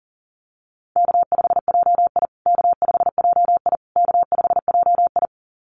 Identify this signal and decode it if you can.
Morse code, heard as a keyed tone, about 40 wpm, 700 Hz. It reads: K5JI K5JI K5JI
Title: USA Call Signs Repeated 3x 40wpm